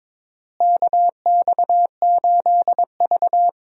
Morse code, heard as a keyed tone, about 22 wpm, 700 Hz. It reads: KX8V
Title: USA Call Signs - 4 Characters - Repeated 2x - 22wpm